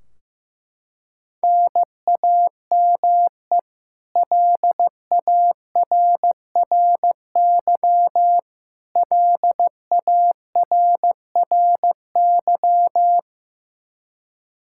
Morse code, heard as a keyed tone, about 16 wpm, 700 Hz. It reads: NAME LARRY LARRY
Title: Practice QSO - Exchange 1 - Beginning - 15wpm